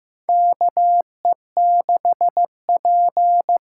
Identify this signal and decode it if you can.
Morse code, heard as a keyed tone, about 15 wpm, 700 Hz. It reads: KE6P
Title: USA Call Signs - 6 Characters - 8x Character Spacing - 15wpm